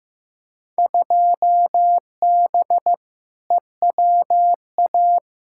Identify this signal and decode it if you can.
Morse code, heard as a keyed tone, about 15 wpm, 700 Hz. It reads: 2B EWA
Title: Field Day Exchanges Encore - 15wpm